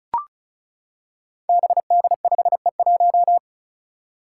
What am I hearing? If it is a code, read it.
Morse code, heard as a keyed tone, about 36 wpm, 700 Hz. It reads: BD5E1
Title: Five Challenging Letter-Number Combinations 35wpm